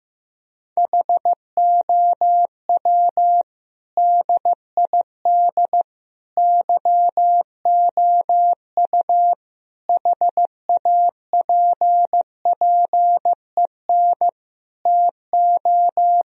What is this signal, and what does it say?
Morse code, heard as a keyed tone, about 15 wpm, 700 Hz. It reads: HOW DID YOU HAPPEN TO
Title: Sentences from Top 1000 Words - Review Part 4 - 15wpm